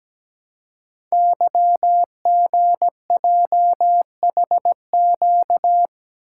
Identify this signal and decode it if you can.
Morse code, heard as a keyed tone, about 17 wpm, 700 Hz. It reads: YGJHQ
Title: Code Groups - 5 Letters - 17wpm